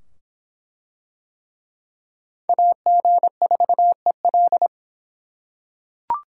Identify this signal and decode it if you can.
Morse code, heard as a keyed tone, about 22 wpm, 700 Hz. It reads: AG4EL